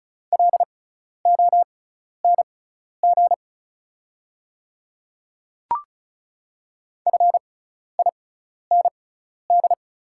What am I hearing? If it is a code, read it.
Morse code, heard as a keyed tone, about 36 wpm, spaced as for 12 wpm, 690 Hz. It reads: LONG FIND
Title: 6x Character Spacing - Top 100 Words - 35wpm